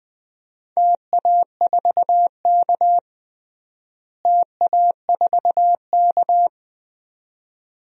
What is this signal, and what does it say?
Morse code, heard as a keyed tone, about 20 wpm, 700 Hz. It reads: TA4K TA4K
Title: DXCC-100 Call Signs - 4 Characters - Repeated 2x - 20wpm